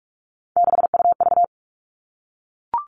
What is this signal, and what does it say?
Morse code, heard as a keyed tone, about 44 wpm, 700 Hz. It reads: 6UV
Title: All Cut Numbers - 3-Digit 45wpm